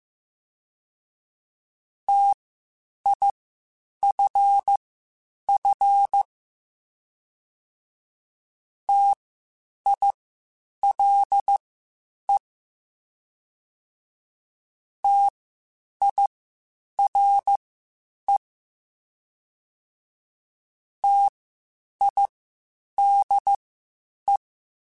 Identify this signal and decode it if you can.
Morse code, heard as a keyed tone, about 15 wpm, spaced as for 8 wpm, 800 Hz. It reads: TIFF TILE TIRE TIDE
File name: Spaced Alphas / Spaced Alphas mp3s / Ts sp @13